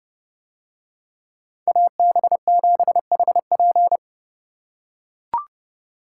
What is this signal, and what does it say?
Morse code, heard as a keyed tone, about 29 wpm, 700 Hz. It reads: AB7HP